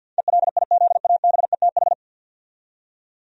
Morse code, heard as a keyed tone, about 46 wpm, 700 Hz. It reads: ELIZABETH
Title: USA Common Names - 50wpm